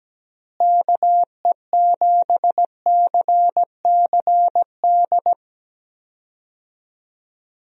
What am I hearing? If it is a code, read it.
Morse code, heard as a keyed tone, about 17 wpm, 700 Hz. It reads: KE7CCD
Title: USA Call Signs - 6 Characters - Repeated 2x - 17wpm